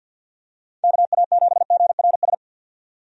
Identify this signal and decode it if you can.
Morse code, heard as a keyed tone, about 53 wpm, 700 Hz. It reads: KA7GRS
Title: USA Call Signs - 6 Characters - 50wpm